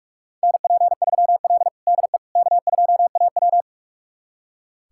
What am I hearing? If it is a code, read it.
Morse code, heard as a keyed tone, about 46 wpm, 700 Hz. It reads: NP3L DE K2AW
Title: USA Callsign DE Callsign - 4 Characters - 45wpm